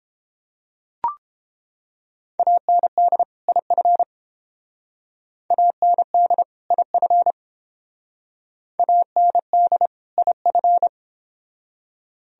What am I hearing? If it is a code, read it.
Morse code, heard as a keyed tone, about 28 wpm, 700 Hz. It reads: AND IF AND IF AND IF